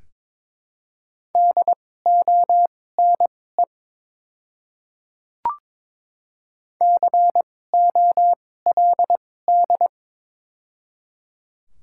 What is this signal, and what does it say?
Morse code, heard as a keyed tone, about 22 wpm, spaced as for 16 wpm, 700 Hz. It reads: DONE COLD
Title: 2x Character Spacing - Top 500 Words - 22wpm